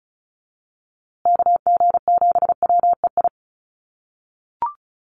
Morse code, heard as a keyed tone, about 34 wpm, 700 Hz. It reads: KG7WEI
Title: USA Call Signs Repeated 2x 35wpm